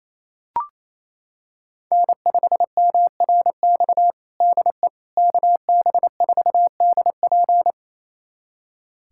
Morse code, heard as a keyed tone, about 28 wpm, 700 Hz. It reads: N5MRX DE KB4DP
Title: USA Callsign DE Callsign - 5 Characters - 28wpm